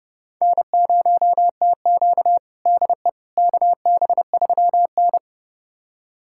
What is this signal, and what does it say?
Morse code, heard as a keyed tone, about 30 wpm, 700 Hz. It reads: N0TQ DE KB3N
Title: USA Callsign DE Callsign - 4 Characters - 30wpm